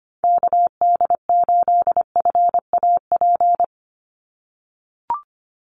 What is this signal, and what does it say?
Morse code, heard as a keyed tone, about 25 wpm, 700 Hz. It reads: KD8FAP
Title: USA Call Signs 25wpm